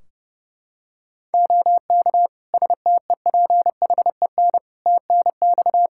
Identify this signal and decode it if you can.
Morse code, heard as a keyed tone, about 30 wpm, 700 Hz. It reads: OK STEPHEN TNX
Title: Practice QSO - Exchange 4 - Ending - 30wpm